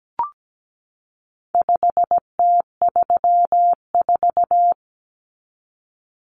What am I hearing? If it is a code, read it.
Morse code, heard as a keyed tone, about 17 wpm, 700 Hz. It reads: 5T34
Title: Common Cut Numbers - 4-Digit 17wpm